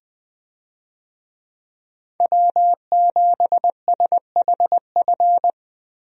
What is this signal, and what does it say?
Morse code, heard as a keyed tone, about 20 wpm, 700 Hz. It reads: W7SHF
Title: USA Call Signs - 5 Characters - Repeated 2x - 20wpm